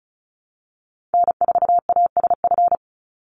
Morse code, heard as a keyed tone, about 34 wpm, 700 Hz. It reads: N4ASF
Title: Call Signs Repeated 3x - 35wpm